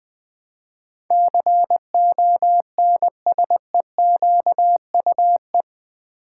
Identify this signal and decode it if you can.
Morse code, heard as a keyed tone, about 20 wpm, 700 Hz. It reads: CONSEQUE
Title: Build A Word - 4 Letters At A Time - 12-Letter Words - 20wpm